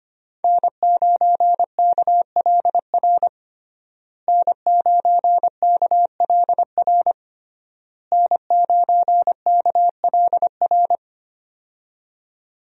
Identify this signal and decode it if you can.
Morse code, heard as a keyed tone, about 25 wpm, 700 Hz. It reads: N9KLR N9KLR N9KLR
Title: USA Call Signs - 5 Characters - Repeated 3x - 25wpm